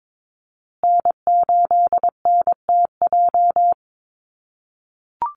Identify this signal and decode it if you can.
Morse code, heard as a keyed tone, about 22 wpm, 700 Hz. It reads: N8NTJ
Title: USA Call Signs Repeated 3x 22wpm